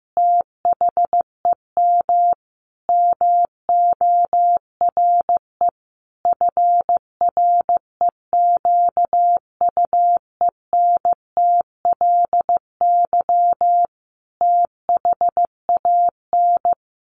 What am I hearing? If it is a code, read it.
Morse code, heard as a keyed tone, about 15 wpm, 700 Hz. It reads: THEM MORE FREQUENTLY THAN
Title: Sentences from Top 1300 Words 15wpm